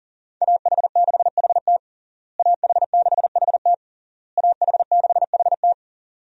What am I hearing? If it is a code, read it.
Morse code, heard as a keyed tone, about 37 wpm, 700 Hz. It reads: AH6HT AH6HT AH6HT
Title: USA Call Signs - 5 Characters - Repeated 3x - 40wpm